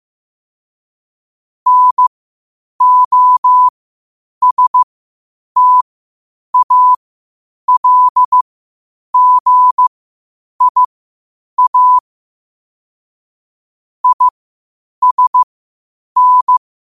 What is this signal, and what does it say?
Morse code, heard as a keyed tone, about 15 wpm, spaced as for 8 wpm, 1000 Hz. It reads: NOSTALGIA ISN
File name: qotdfiles / 8wpm